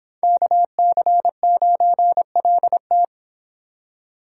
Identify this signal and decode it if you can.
Morse code, heard as a keyed tone, about 26 wpm, 700 Hz. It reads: KC9LT